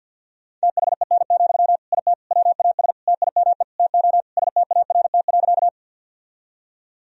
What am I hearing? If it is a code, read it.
Morse code, heard as a keyed tone, about 52 wpm, 700 Hz. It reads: THEN, IT WAS TIME TO START.
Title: Sentences from Top 1000 Words - Review Part 1 - 50wpm